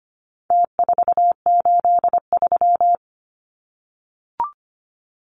Morse code, heard as a keyed tone, about 25 wpm, 700 Hz. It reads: T483
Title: Numbers - Contest Style - 4-Digit 25wpm